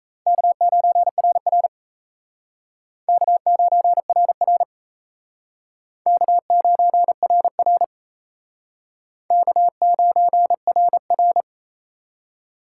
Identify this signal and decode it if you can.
Morse code, heard as a keyed tone, about 34 wpm, 700 Hz. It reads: K9RR K9RR K9RR K9RR
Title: Speed Racing - USA Call Signs 28wpm